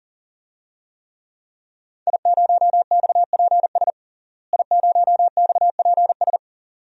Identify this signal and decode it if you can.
Morse code, heard as a keyed tone, about 40 wpm, 700 Hz. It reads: I0XPS I0XPS
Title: DXCC-100 Call Signs - 5 Characters - Repeated 2x - 40wpm